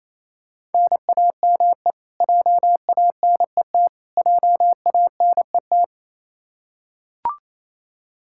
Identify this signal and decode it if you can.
Morse code, heard as a keyed tone, about 28 wpm, 700 Hz. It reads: NAME JANET JANET
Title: Practice QSO - Exchange 1 - Beginning - 28wpm